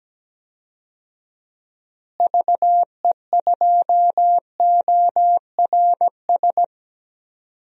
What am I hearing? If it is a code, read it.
Morse code, heard as a keyed tone, about 17 wpm, 700 Hz. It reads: VE2ORS
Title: DXCC-100 Call Signs - 6 Characters - Repeated 3x - 17wpm